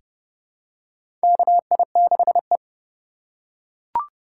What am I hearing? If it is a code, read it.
Morse code, heard as a keyed tone, about 30 wpm, 700 Hz. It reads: KI6E
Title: USA Call Signs - 4 Characters - 4x Character Spacing - 30wpm